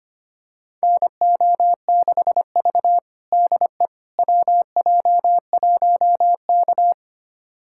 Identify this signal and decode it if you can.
Morse code, heard as a keyed tone, about 25 wpm, 700 Hz. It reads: NO6V DE WJ1K
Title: USA Callsign DE Callsign - 4 Characters - 25wpm